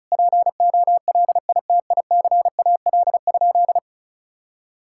Morse code, heard as a keyed tone, about 34 wpm, 700 Hz. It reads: POLITICAL?
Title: Sentences from Top 1000 Words - Review Part 5 - 35wpm